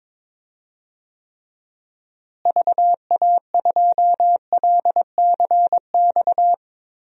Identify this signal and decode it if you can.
Morse code, heard as a keyed tone, about 22 wpm, 700 Hz. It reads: VA2LCX
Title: DXCC-100 Call Signs - 6 Characters - Repeated 2x - 22wpm